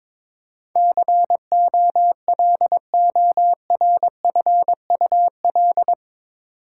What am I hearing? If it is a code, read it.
Morse code, heard as a keyed tone, about 22 wpm, 700 Hz. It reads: COLORFUL
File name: practice / 22wpm / Eight Letter Words - 22wpm